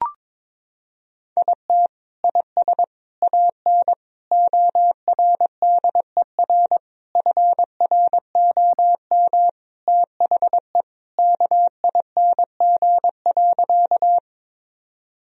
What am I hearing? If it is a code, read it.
Morse code, heard as a keyed tone, about 22 wpm, 700 Hz. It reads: IT IS AN ORDER FROM THE KING.
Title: Sentences from Top 1000 Words - Review Part 2 - 22wpm